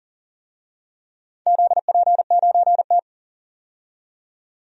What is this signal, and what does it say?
Morse code, heard as a keyed tone, about 39 wpm, 700 Hz. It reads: ZP9T
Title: DXCC-100 Call Signs - 4 Characters - 40wpm